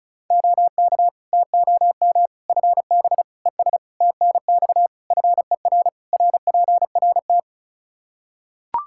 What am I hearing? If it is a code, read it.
Morse code, heard as a keyed tone, about 34 wpm, 700 Hz. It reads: OK TOM FB ES TNX FER RPRT
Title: Practice QSO - Exchange 2 - Info - 35wpm